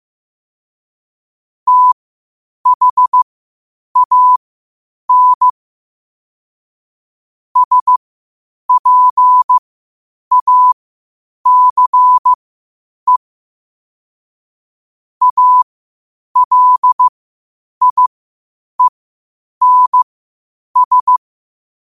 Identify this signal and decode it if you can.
Morse code, heard as a keyed tone, about 15 wpm, spaced as for 8 wpm, 1000 Hz. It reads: THAN SPACE ALIENS
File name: qotdfiles / 8wpm